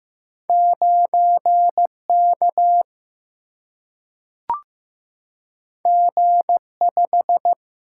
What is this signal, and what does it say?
Morse code, heard as a keyed tone, about 15 wpm, 700 Hz. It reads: 9K G5
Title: Single-Letter Single-Number 15wpm